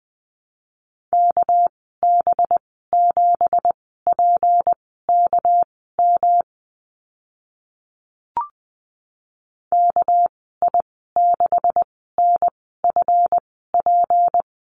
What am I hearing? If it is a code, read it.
Morse code, heard as a keyed tone, about 20 wpm, spaced as for 14 wpm, 700 Hz. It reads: KB7PKM KI6NFP
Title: 200 Lesson 32g - Letter k - Call Signs 20wpm